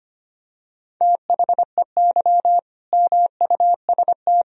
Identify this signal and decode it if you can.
Morse code, heard as a keyed tone, about 25 wpm, 700 Hz. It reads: THEY MUST
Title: Sentences from Top 1000 Words - Review Part 1 - 25wpm